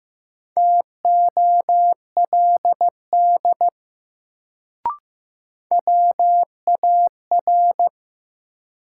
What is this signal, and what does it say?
Morse code, heard as a keyed tone, about 15 wpm, 700 Hz. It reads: TOLD WAR
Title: Sentences from Top 400 Words - New Word Drill - 15wpm